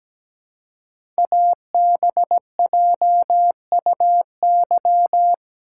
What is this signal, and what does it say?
Morse code, heard as a keyed tone, about 17 wpm, 700 Hz. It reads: ABJUY
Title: Code Groups - 5 Letters - 17wpm